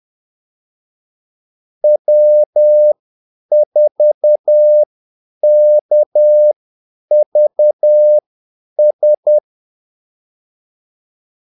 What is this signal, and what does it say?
Morse code, heard as a keyed tone, about 10 wpm, spaced as for 8 wpm, 600 Hz. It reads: W4KVS
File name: Callsigns-10-8wpm